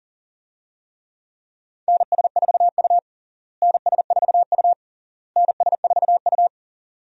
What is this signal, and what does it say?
Morse code, heard as a keyed tone, about 40 wpm, 700 Hz. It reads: NS4U NS4U NS4U
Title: USA Call Signs - 4 Characters - Repeated 3x - 40wpm